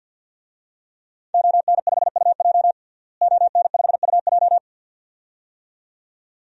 Morse code, heard as a keyed tone, about 52 wpm, 700 Hz. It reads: ON5UJ ON5UJ
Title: DXCC-100 Call Signs - 5 Characters - Repeated 2x - 50wpm